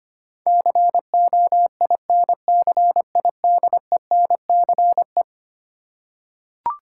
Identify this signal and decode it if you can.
Morse code, heard as a keyed tone, about 25 wpm, 700 Hz. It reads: COINCIDENCE
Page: morse-code-ninja-practice-25wpm